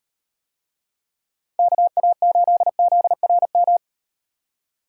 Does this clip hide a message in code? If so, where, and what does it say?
Yes: Morse code from 1.6 to 3.8 s, heard as a keyed tone, about 35 wpm, 700 Hz. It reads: KA8ZRM